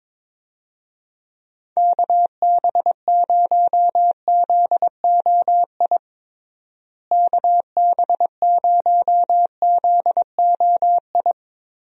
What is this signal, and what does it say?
Morse code, heard as a keyed tone, about 21 wpm, 700 Hz. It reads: KB0ZOI KB0ZOI
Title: USA Call Signs - 6 Characters - Repeated 2x - 22wpm